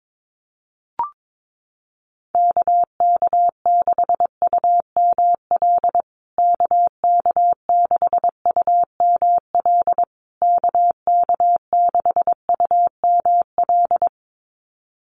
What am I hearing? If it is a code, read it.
Morse code, heard as a keyed tone, about 22 wpm, 700 Hz. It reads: KK6UML KK6UML KK6UML
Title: USA Call Signs Repeated 3x 22wpm